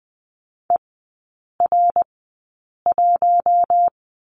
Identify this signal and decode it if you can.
Morse code, heard as a keyed tone, about 20 wpm, spaced as for 8 wpm, 700 Hz. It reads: ER1
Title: 127 Lesson 21e - Number 6 - Three Characters 20wpm